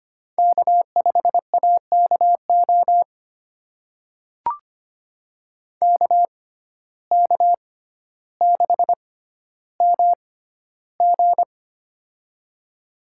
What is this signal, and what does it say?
Morse code, heard as a keyed tone, about 25 wpm, spaced as for 13 wpm, 700 Hz. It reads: K5AKO KK6MG
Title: USA Call Signs - 5 Characters - 6x Character Spacing - 25wpm